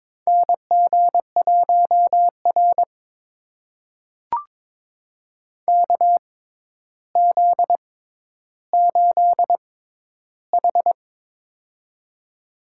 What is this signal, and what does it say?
Morse code, heard as a keyed tone, about 22 wpm, spaced as for 11 wpm, 700 Hz. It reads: NG1R KZ8H
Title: USA Call Signs - 4 Characters - 6x Character Spacing - 22wpm